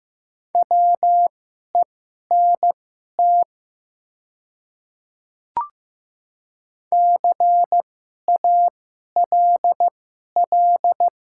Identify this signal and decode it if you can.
Morse code, heard as a keyed tone, about 15 wpm, spaced as for 11 wpm, 700 Hz. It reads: WENT CALL
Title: 2x Character Spacing - Top 200 Words - 15wpm